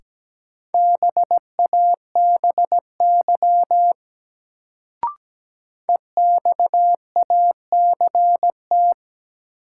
Morse code, heard as a keyed tone, about 17 wpm, 700 Hz. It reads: BABY EXACT
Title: Sentences from Top 1300 Words - New Word Drill Alt - 17wpm